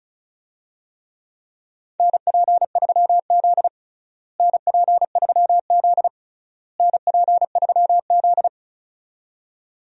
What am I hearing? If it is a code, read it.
Morse code, heard as a keyed tone, about 34 wpm, 700 Hz. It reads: NP3Z NP3Z NP3Z
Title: USA Call Signs - 4 Characters - Repeated 3x - 35wpm